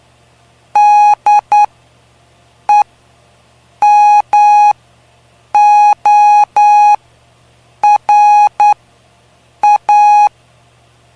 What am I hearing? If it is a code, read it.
Morse code, heard as a keyed tone, about 9 wpm, spaced as for 6 wpm, 830 Hz. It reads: DEMORA
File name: cw / py2lcd / 05-PY2LCD-Li��o 5